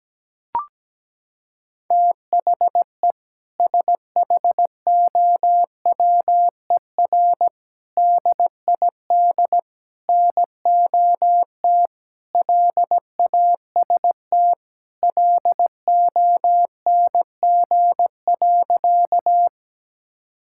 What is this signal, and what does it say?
Morse code, heard as a keyed tone, about 17 wpm, 700 Hz. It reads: THE SHOWER DID NOT LAST LONG.
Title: Sentences from Top 1600 Words 17wpm